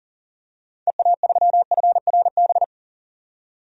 Morse code, heard as a keyed tone, about 40 wpm, 700 Hz. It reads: EA3FRB
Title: DXCC-100 Call Signs - 6 Characters - 40wpm